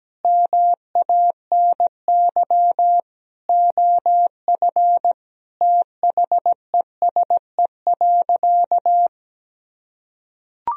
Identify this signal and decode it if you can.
Morse code, heard as a keyed tone, about 17 wpm, 700 Hz. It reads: MANY OF THESE.
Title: Sentences from Top 1000 Words - Review Part 1 - 17wpm